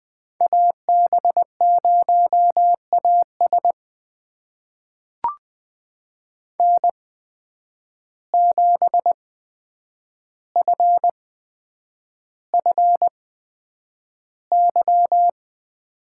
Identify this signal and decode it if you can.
Morse code, heard as a keyed tone, about 20 wpm, spaced as for 9 wpm, 700 Hz. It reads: AB0AS N7FFY
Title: USA Call Signs - 5 Characters - 8x Character Spacing - 20wpm